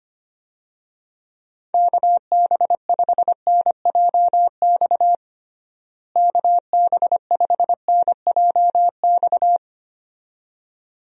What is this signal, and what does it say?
Morse code, heard as a keyed tone, about 25 wpm, 700 Hz. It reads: KB5NJX KB5NJX
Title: USA Call Signs - 6 Characters - Repeated 2x - 25wpm